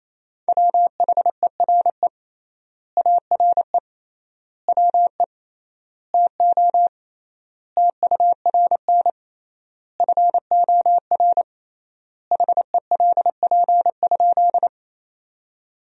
Morse code, heard as a keyed tone, about 28 wpm, 700 Hz. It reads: WHERE ARE WE TO TURN FOR HELP?
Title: 3x Word-Spacing - Sentences from Top 200 Words - 28wpm